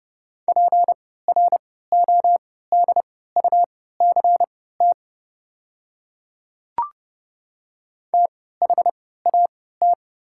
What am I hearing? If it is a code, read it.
Morse code, heard as a keyed tone, about 31 wpm, spaced as for 17 wpm, 700 Hz. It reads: PRODUCT THAT
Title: 3x Character Spacing - Top 500 Words - 30wpm